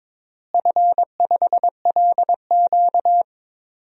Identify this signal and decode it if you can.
Morse code, heard as a keyed tone, about 23 wpm, 700 Hz. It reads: F5LQ
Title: DXCC-100 Call Signs - 4 Characters - Repeated 3x - 22wpm